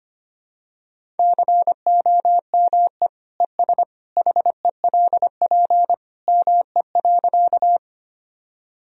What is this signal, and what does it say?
Morse code, heard as a keyed tone, about 25 wpm, 700 Hz. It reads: COME ES HELP ME.
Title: Sentences from Top 200 Words and Abbreviations 25wpm